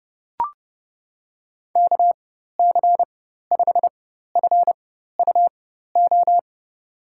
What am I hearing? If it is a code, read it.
Morse code, heard as a keyed tone, about 30 wpm, spaced as for 14 wpm, 700 Hz. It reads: KC5FUO
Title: USA Call Signs - 4x Character Spacing - 30wpm